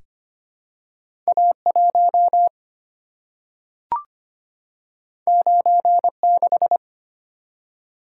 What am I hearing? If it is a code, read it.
Morse code, heard as a keyed tone, about 25 wpm, 700 Hz. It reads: A1 96
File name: Two Letter-Number Combinations - 25wpm